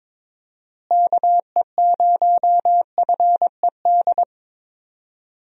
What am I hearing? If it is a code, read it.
Morse code, heard as a keyed tone, about 22 wpm, 700 Hz. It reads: KE0FED
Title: USA Call Signs - 6 Characters - Repeated 3x - 22wpm